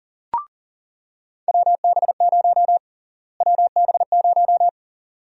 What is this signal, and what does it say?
Morse code, heard as a keyed tone, about 40 wpm, 700 Hz. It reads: WB0 WB0
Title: USA Call Sign Prefixes - Repeated 2x - 40wpm